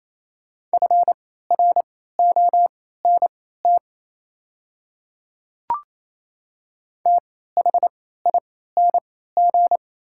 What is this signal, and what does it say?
Morse code, heard as a keyed tone, about 27 wpm, spaced as for 16 wpm, 700 Hz. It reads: FRONT THING
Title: 3x Character Spacing - Top 500 Words - 28wpm